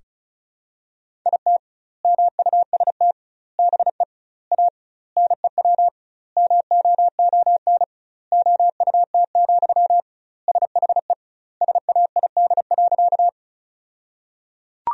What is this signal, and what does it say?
Morse code, heard as a keyed tone, about 36 wpm, 700 Hz. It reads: IT MUST BE A NEW MOON OUT, SHE SAID.
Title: 2x Word-Spacing - Sentences from Top 500 Words - 35wpm